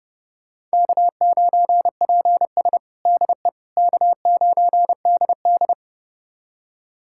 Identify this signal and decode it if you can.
Morse code, heard as a keyed tone, about 30 wpm, 700 Hz. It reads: K9PS DE K9DD
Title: USA Callsign DE Callsign - 4 Characters - 30wpm